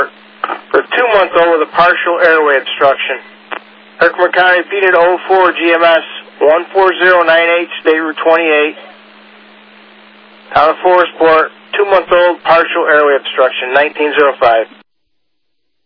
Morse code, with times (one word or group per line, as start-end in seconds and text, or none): none